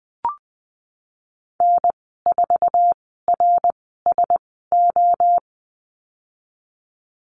Morse code, heard as a keyed tone, about 19 wpm, spaced as for 14 wpm, 700 Hz. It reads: N4RSO
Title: 069 Lesson 12g - Letter d - Call Signs 20wpm